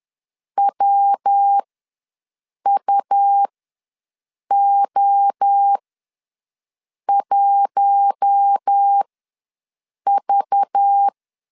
Morse code, heard as a keyed tone, about 10 wpm, spaced as for 6 wpm, 800 Hz. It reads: WUO1V
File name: LEZIONI / 5 LBVC / 5 LBV ws 10wpm 3645